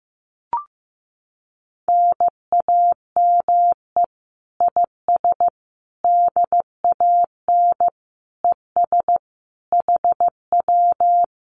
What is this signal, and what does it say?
Morse code, heard as a keyed tone, about 15 wpm, 700 Hz. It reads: NAME IS DAN ES HW
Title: Common QSO Phrases Encore 15wpm